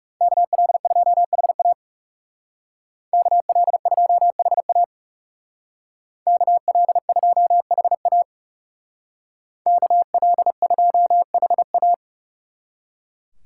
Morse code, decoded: KL2HA KL2HA KL2HA KL2HA

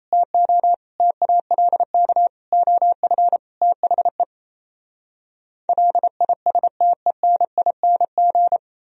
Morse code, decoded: TO TALK OF THE LISTENING

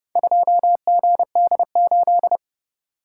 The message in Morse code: 2GD8

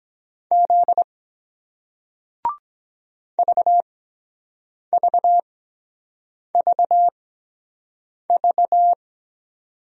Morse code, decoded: Z VVVV

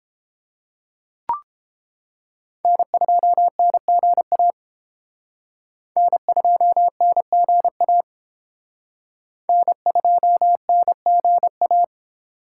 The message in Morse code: N2NGA N2NGA N2NGA

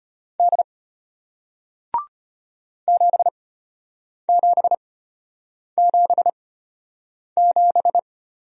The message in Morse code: D 7777

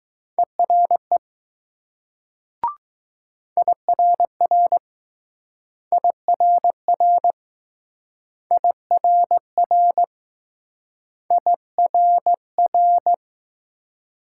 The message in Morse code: ERE IRR IRR IRR IRR

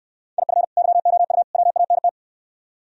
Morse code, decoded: IF YOU WANT

